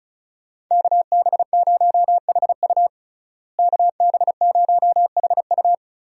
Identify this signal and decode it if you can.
Morse code, heard as a keyed tone, about 36 wpm, 700 Hz. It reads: KB0HU KB0HU
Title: USA Call Signs - 5 Characters - Repeated 2x - 35wpm